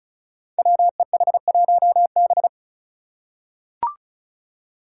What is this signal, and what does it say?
Morse code, heard as a keyed tone, about 36 wpm, 700 Hz. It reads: WEH1B
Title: Five Challenging Letter-Number Combinations 35wpm